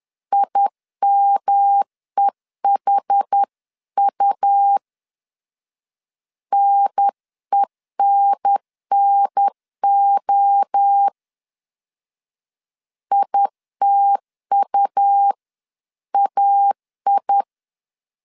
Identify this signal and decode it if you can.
Morse code, heard as a keyed tone, about 10 wpm, 800 Hz. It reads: IMEHU NENNO ITUAI
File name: CW / DET / 1 3 1500C S C 10wpm 3335